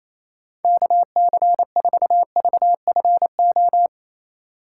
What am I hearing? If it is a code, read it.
Morse code, heard as a keyed tone, about 28 wpm, 700 Hz. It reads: KC4VFO
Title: USA Call Signs - 6 Characters - 2x Character Spacing - 28wpm